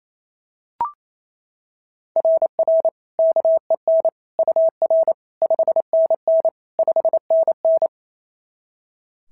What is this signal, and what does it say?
Morse code, heard as a keyed tone, about 28 wpm, 690 Hz. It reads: RR KEN UR 5NN 5NN
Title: SOTA QSO Practice 28wpm